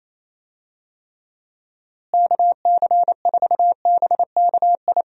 Morse code, heard as a keyed tone, about 28 wpm, 700 Hz. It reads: KC4BKI
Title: USA Call Signs - 6 Characters - Repeated 3x - 28wpm